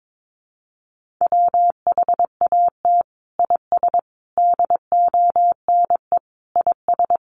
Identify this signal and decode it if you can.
Morse code, heard as a keyed tone, about 22 wpm, 700 Hz. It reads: WHAT IS DONE IS
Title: Sentences from Top 500 Words 22wpm